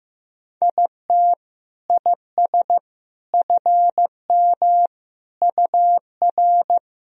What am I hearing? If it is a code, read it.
Morse code, heard as a keyed tone, about 15 wpm, 700 Hz. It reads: IT IS FM UR
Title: Sentences from Top 300 Words and Abbreviations 15wpm